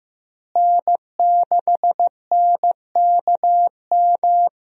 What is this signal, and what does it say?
Morse code, heard as a keyed tone, about 16 wpm, 700 Hz. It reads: N6NKM
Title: USA Call Signs - 5 Characters - Repeated 2x - 15wpm